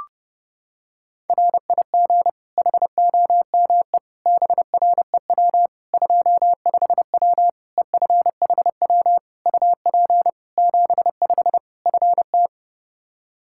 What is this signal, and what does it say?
Morse code, heard as a keyed tone, about 30 wpm, 700 Hz. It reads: RIG HOME BREW 25W EFHW UP 75 FT
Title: K1USN SST QSO Practice 30wpm